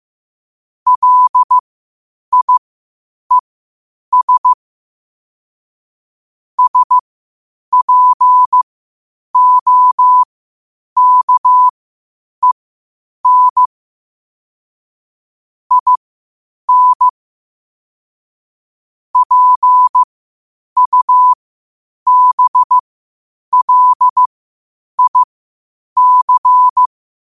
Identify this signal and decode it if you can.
Morse code, heard as a keyed tone, about 15 wpm, spaced as for 8 wpm, 1000 Hz. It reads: LIES SPOKEN IN PUBLIC